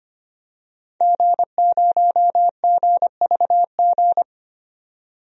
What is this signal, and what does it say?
Morse code, heard as a keyed tone, about 25 wpm, 700 Hz. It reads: G0GVG
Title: DXCC-100 Call Signs - 5 Characters - 25wpm